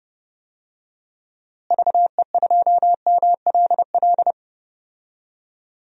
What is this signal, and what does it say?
Morse code, heard as a keyed tone, about 30 wpm, 700 Hz. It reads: VE2MLL
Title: DXCC-100 Call Signs - 6 Characters - Repeated 2x - 30wpm